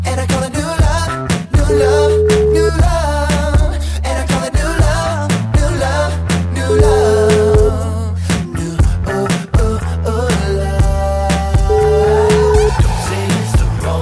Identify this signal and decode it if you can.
Ringback tone (Europe, 425 Hz), with noise.